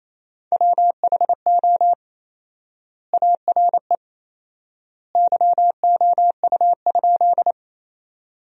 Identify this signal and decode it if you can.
Morse code, heard as a keyed tone, about 28 wpm, 700 Hz. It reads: WHO ARE YOU?
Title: 4x Word-Spacing - Sentences from Top 100 Words - 28wpm